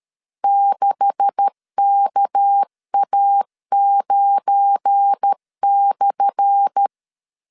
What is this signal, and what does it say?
Morse code, heard as a keyed tone, about 13 wpm, 800 Hz. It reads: 6KA9/
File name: LEZIONI / 10 SIGNS / 10B INT 12wpm 2414